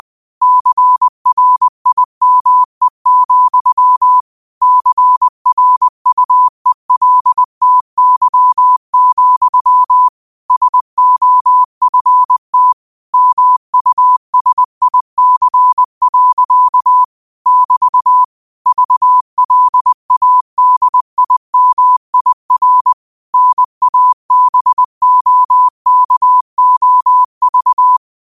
Morse code, heard as a keyed tone, about 20 wpm, 1000 Hz. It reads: CRIME, CRUELTY, SOFT MUSIC. <BT> VLADIMIR NABOKOV